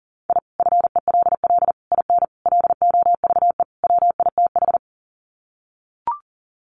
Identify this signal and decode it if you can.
Morse code, heard as a keyed tone, about 40 wpm, 700 Hz. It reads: I FELL IN LOVE WITH
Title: Sets of 5 Words Encore 40wpm